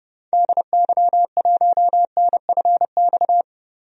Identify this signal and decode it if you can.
Morse code, heard as a keyed tone, about 30 wpm, 700 Hz. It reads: DY1NFX
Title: DXCC-100 Call Signs - 6 Characters - 30wpm